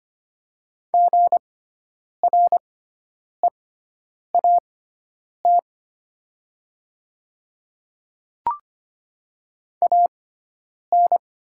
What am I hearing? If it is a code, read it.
Morse code, heard as a keyed tone, about 25 wpm, spaced as for 9 wpm, 700 Hz. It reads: GREAT AN
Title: 6x Character Spacing - Top 300 Words - 25wpm